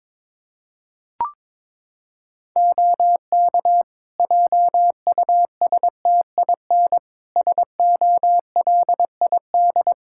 Morse code, heard as a keyed tone, about 22 wpm, 700 Hz. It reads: OK JUSTIN SOLID
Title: Practice QSO - Exchange 3 - More Info - 22wpm